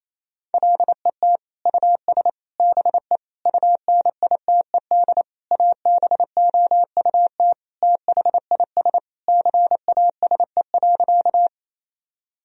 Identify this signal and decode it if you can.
Morse code, heard as a keyed tone, about 28 wpm, 700 Hz. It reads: LET US BE UNITED ABOUT THIS CASE.
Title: Sentences from Top 1000 Words - Review Part 4 - 28wpm